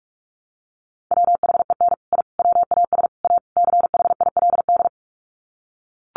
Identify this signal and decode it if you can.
Morse code, heard as a keyed tone, about 44 wpm, 700 Hz. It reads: WHEN I WAS A CHILD
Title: Sets of 5 Words Encore 45wpm